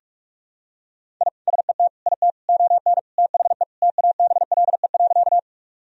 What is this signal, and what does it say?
Morse code, heard as a keyed tone, about 46 wpm, 700 Hz. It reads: I SET IT ON THE TABLE.
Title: Sentences from Top 1000 Words - Review Part 2 - 45wpm